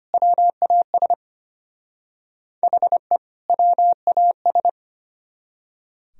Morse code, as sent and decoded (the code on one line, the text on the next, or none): .-- .- ... / .... / .-- .- ...
WAS HE WAS